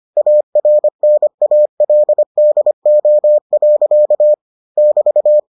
Morse code, heard as a keyed tone, about 25 wpm, 600 Hz. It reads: ARNALDO. <BT>